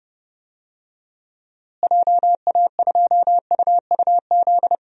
Morse code, heard as a keyed tone, about 30 wpm, 700 Hz. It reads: JA2UUZ
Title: DXCC-100 Call Signs - 6 Characters - Repeated 3x - 30wpm